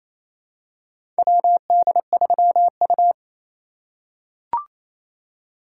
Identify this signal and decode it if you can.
Morse code, heard as a keyed tone, about 27 wpm, 700 Hz. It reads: WD3U
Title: USA Call Signs - 4 Characters - 3x Character Spacing - 28wpm